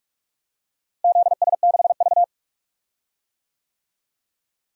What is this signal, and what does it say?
Morse code, heard as a keyed tone, about 43 wpm, 700 Hz. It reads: ZS6V
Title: DXCC-100 Call Signs - 4 Characters - Repeated 2x - 45wpm